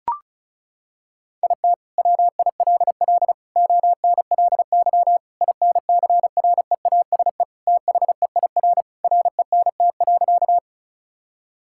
Morse code, heard as a keyed tone, about 36 wpm, 700 Hz. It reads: IT WILL ONLY INCREASE THEIR RENT.
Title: Sentences from Top 1000 Words - Review Part 5 - 35wpm